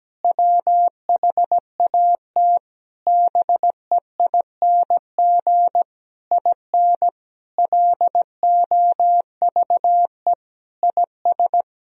Morse code, heard as a keyed tone, about 17 wpm, 700 Hz. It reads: WHAT BEING IN LOVE IS
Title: Sentences from Top 1000 Words - Review Part 3 - 17wpm